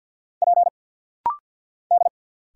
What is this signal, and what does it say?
Morse code, heard as a keyed tone, about 50 wpm, 700 Hz. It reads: P D
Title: Single Letter-Number Alt - Rapid-Fire 50wpm